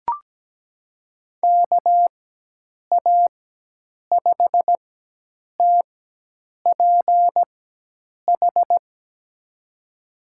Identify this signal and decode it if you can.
Morse code, heard as a keyed tone, about 17 wpm, spaced as for 8 wpm, 700 Hz. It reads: KA5TPH
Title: USA Call Signs - 6 Characters - 4x Character Spacing - 17wpm